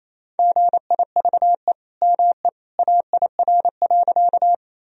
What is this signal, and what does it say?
Morse code, heard as a keyed tone, about 28 wpm, 700 Hz. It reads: GIVE ME AIR.
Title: Sentences from Top 1000 Words - Review Part 1 - 28wpm